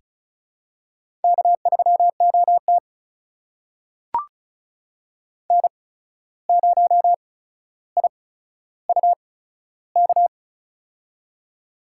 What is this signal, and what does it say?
Morse code, heard as a keyed tone, about 36 wpm, spaced as for 14 wpm, 700 Hz. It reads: K3OT N0IUK
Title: USA Call Signs - 8x Character Spacing - 35wpm